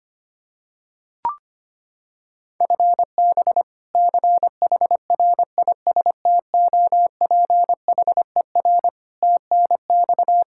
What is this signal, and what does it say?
Morse code, heard as a keyed tone, about 25 wpm, 700 Hz. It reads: FB CHRISTOPHER TNX
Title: K1USN SST QSO Practice 25wpm